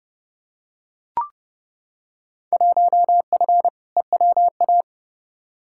Morse code, heard as a keyed tone, about 32 wpm, 700 Hz. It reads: 1F EWA
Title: Field Day Exchanges Encore - 30wpm